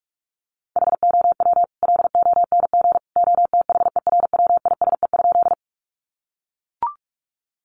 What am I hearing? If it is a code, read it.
Morse code, heard as a keyed tone, about 44 wpm, 700 Hz. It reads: HOW LONG OTHERWISE?
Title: Sentences from Top 1200 Words 45wpm